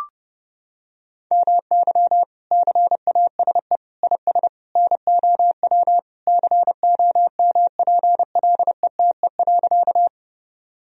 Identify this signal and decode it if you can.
Morse code, heard as a keyed tone, about 30 wpm, 700 Hz. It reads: MY CASE IS NOW COMPLETE.
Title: Sentences from Top 1000 Words - Review Part 2 - 30wpm